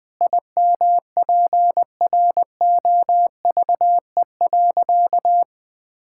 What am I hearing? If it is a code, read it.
Morse code, heard as a keyed tone, about 21 wpm, 700 Hz. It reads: IMPROVE.
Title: Sentences from Top 1000 Words - Review Part 5 - 20wpm